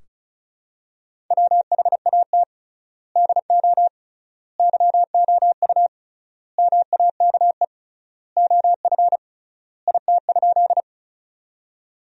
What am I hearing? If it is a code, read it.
Morse code, heard as a keyed tone, about 36 wpm, 700 Hz. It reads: WHAT DO YOU MAKE OF IT?
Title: 3x Word-Spacing - Sentences from Top 100 Words - 35wpm